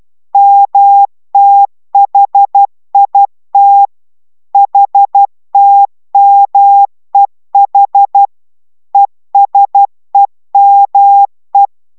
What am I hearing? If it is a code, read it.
Morse code, heard as a keyed tone, about 12 wpm, 800 Hz. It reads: MTHIT HTMEH ESEME